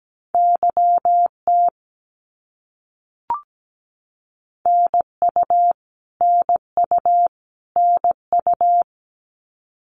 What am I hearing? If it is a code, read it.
Morse code, heard as a keyed tone, about 17 wpm, 700 Hz. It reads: YT NU NU NU
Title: Canadian Provinces and Territories - Repeated 3x - 17wpm